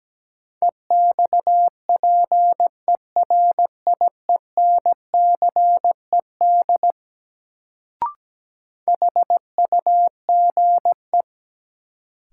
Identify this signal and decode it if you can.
Morse code, heard as a keyed tone, about 17 wpm, 700 Hz. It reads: EXPERIENCED HUGE
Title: Sentences from Top 1100 Words - New Word Drill - 17wpm